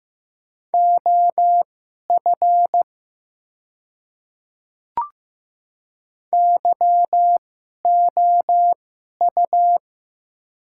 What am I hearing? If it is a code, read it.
Morse code, heard as a keyed tone, about 15 wpm, spaced as for 11 wpm, 700 Hz. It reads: OF YOU
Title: 2x Character Spacing - Top 100 Words - 15wpm